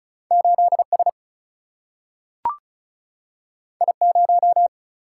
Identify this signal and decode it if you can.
Morse code, heard as a keyed tone, about 36 wpm, 700 Hz. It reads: 8S I0
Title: Single-Letter Single-Number 35wpm